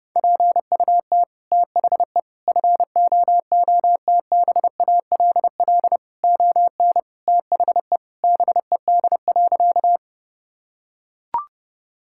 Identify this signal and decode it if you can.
Morse code, heard as a keyed tone, about 30 wpm, 700 Hz. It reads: PUT THE FOOTBALL ON THE BED.
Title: Sentences from Top 1600 Words 30wpm